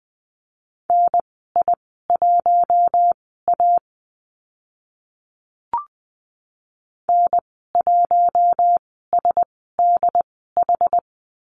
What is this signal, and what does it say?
Morse code, heard as a keyed tone, about 20 wpm, spaced as for 14 wpm, 700 Hz. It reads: NI1A N1SDH
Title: 069 Lesson 12g - Letter d - Call Signs 20wpm